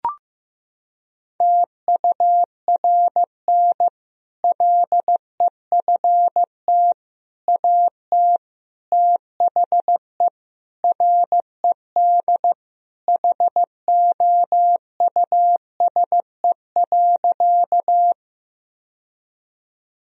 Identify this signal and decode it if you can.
Morse code, heard as a keyed tone, about 15 wpm, 700 Hz. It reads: TURN LEFT AT THE RED HOUSE.
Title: Sentences from Top 1000 Words - Review Part 2 - 15wpm